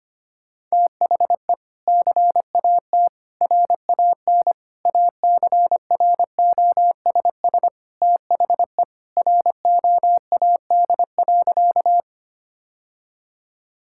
Morse code, heard as a keyed tone, about 25 wpm, 700 Hz. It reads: THE CAT RAN ACROSS THE ROAD.
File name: practice / 25wpm / Sentences with 2-Word Phrases - 25wpm